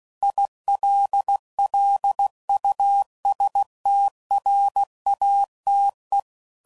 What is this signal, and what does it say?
Morse code, heard as a keyed tone, about 16 wpm, 800 Hz. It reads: ILLUSTRATE
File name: CW Code Groups @15WPM / Technical Subjects / AND Gate